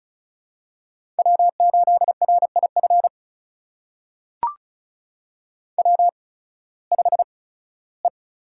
Morse code, heard as a keyed tone, about 36 wpm, spaced as for 19 wpm, 700 Hz. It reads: W8RIF W5E
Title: USA Call Signs - 5 Characters - 8x Character Spacing - 35wpm